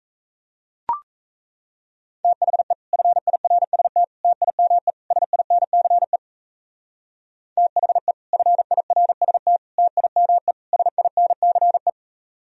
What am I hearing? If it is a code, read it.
Morse code, heard as a keyed tone, about 34 wpm, 700 Hz. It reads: THE FIRST TIME SINCE THE FIRST TIME SINCE
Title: Speed-Racing - Sets of 4 Words 28wpm